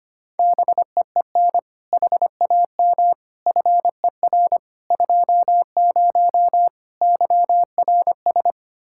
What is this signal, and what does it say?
Morse code, heard as a keyed tone, about 25 wpm, 700 Hz. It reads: BEEN HAM FER 20 YRS
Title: K1USN SST QSO Practice 25wpm